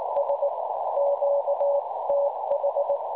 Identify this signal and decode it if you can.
Morse code, heard as a keyed tone, about 19 wpm, 570 Hz, with noise. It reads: S QTH
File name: cwronde / websdr recording .6kHz